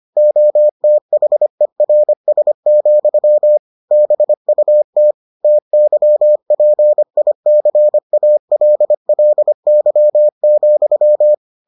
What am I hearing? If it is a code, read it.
Morse code, heard as a keyed tone, about 25 wpm, 600 Hz. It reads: OTHERS, BUT TYPICALLY,